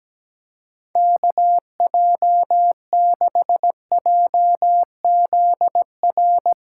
Morse code, heard as a keyed tone, about 17 wpm, 700 Hz. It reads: KJ6JZR
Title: USA Callsign DE Callsign 17wpm